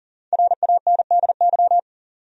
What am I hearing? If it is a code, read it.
Morse code, heard as a keyed tone, about 37 wpm, 700 Hz. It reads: RANDY
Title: USA Common Names - Max 5 Letters - 40wpm